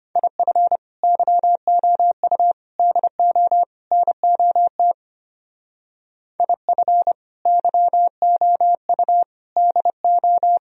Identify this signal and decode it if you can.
Morse code, heard as a keyed tone, about 33 wpm, 700 Hz. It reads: IF YOU DO NOT IF YOU DO